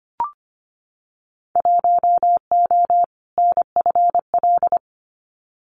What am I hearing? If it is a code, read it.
Morse code, heard as a keyed tone, about 25 wpm, 700 Hz. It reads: 1O NFL
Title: Winter Field Day Exchanges 25wpm